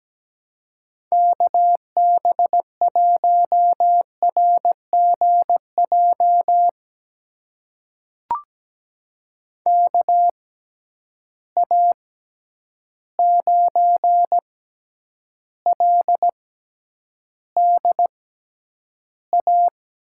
Morse code, heard as a keyed tone, about 17 wpm, spaced as for 9 wpm, 700 Hz. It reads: KB1RGJ KA9LDA
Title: USA Call Signs - 6 Characters - 6x Character Spacing - 17wpm